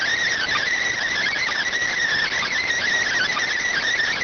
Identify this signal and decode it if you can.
Emergency Alert System (SAME) tones, heard as harsh buzzing bursts. It is heard from 1.1 to 3.9 s.